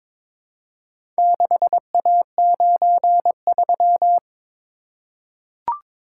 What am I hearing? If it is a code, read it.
Morse code, heard as a keyed tone, about 22 wpm, 700 Hz. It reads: 6A93
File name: practice / 22wpm / Four Letter-Number Combinations - 22wpm